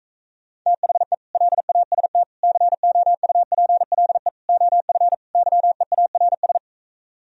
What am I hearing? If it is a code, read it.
Morse code, heard as a keyed tone, about 45 wpm, 700 Hz. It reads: THE LAST COUPLE OF YEARS